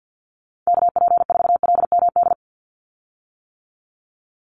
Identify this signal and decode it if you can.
Morse code, heard as a keyed tone, about 50 wpm, 700 Hz. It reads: KP4LMD
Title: USA Call Signs 50wpm